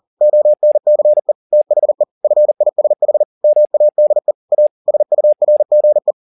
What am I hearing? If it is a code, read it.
Morse code, heard as a keyed tone, about 40 wpm, 600 Hz. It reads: ONCE THE FISH MADE A SURGE